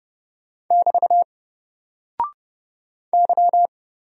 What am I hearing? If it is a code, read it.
Morse code, heard as a keyed tone, about 29 wpm, 700 Hz. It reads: <BT> Y
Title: Single Character (Letter:Number:Punctuation:Prosign) 30wpm